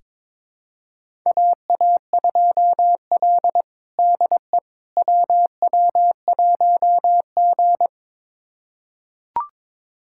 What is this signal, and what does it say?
Morse code, heard as a keyed tone, about 22 wpm, 700 Hz. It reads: AA2L DE WW1G
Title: USA Callsign DE Callsign - 4 Characters - 22wpm